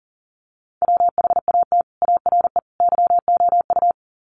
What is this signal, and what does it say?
Morse code, heard as a keyed tone, about 40 wpm, 700 Hz. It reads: WHAT ARE YOU
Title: Sets of 3 Words 40wpm